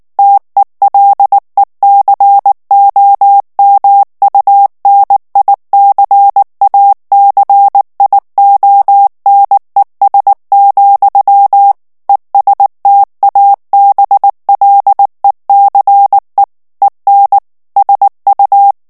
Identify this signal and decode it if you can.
Morse code, heard as a keyed tone, about 19 wpm, 800 Hz. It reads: TELECOMUNICACIONES, ESTABLECE EN SU
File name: ESPAGNOL TEXTE EN CLAIR 2 VITESSE 18 MOTS MINUTES